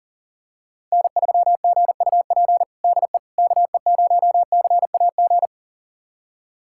Morse code, heard as a keyed tone, about 40 wpm, 700 Hz. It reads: N3GUP DE KE0CAG
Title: USA Callsign DE Callsign 40wpm